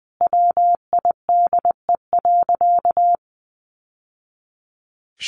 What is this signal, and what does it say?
Morse code, heard as a keyed tone, about 20 wpm, 700 Hz. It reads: WIDE.
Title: Sentences from Top 1200 Words 20wpm